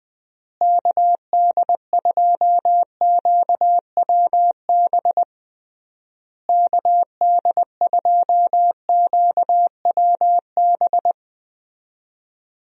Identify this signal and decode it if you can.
Morse code, heard as a keyed tone, about 21 wpm, 700 Hz. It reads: KD2QWB KD2QWB
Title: USA Call Signs - 6 Characters - Repeated 3x - 20wpm